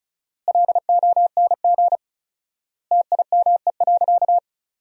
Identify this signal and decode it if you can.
Morse code, heard as a keyed tone, about 34 wpm, 700 Hz. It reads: LONG TIME.
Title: 4x Word-Spacing - Sentences from Top 400 Words - 35wpm